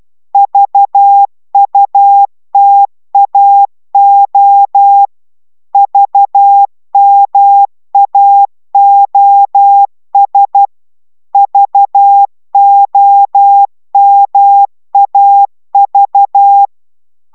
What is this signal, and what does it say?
Morse code, heard as a keyed tone, about 12 wpm, 800 Hz. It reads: VUTAO VMAOS VOMAV